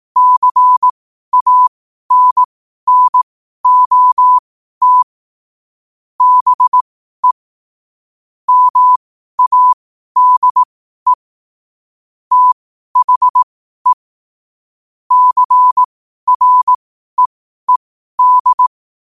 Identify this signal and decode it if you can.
Morse code, heard as a keyed tone, about 18 wpm, spaced as for 12 wpm, 1000 Hz. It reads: CANNOT BE MADE THE CREED
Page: New quotes every day in morse code at 12 Words per minute.